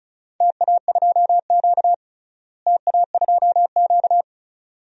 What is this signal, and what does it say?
Morse code, heard as a keyed tone, about 36 wpm, 700 Hz. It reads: TA2Q TA2Q
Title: DXCC-100 Call Signs - 4 Characters - Repeated 2x - 35wpm